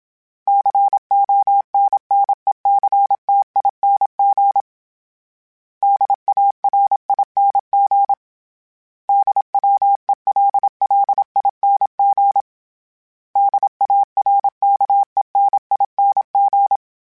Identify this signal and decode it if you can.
Morse code, heard as a keyed tone, about 27 wpm, 800 Hz. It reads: CONNECTING DARING DWELLING DARKENING
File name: Special INGs-mp3s / Special INGs-mp3s / Sngl Ing @25